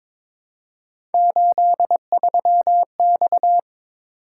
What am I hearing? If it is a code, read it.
Morse code, heard as a keyed tone, about 21 wpm, 700 Hz. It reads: 83X
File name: practice / 22wpm / Three Letter-Number Combinations - 22wpm